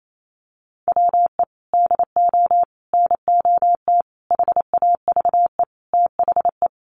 Morse code, heard as a keyed tone, about 28 wpm, 700 Hz. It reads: WE DO NOT HAVE THE
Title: Sentences from Top 1200 Words 28wpm